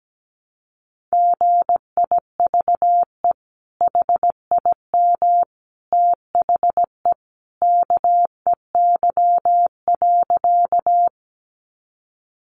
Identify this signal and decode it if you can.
Morse code, heard as a keyed tone, about 17 wpm, 700 Hz. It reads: GIVE HIM THE KEY.
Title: Sentences from Top 600 Words 17wpm